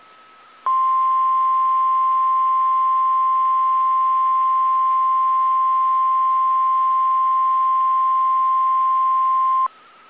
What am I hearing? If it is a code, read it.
Emergency Alert System (SAME) tones (the 1050 Hz attention tone). It is heard from 0.6 to 9.7 s.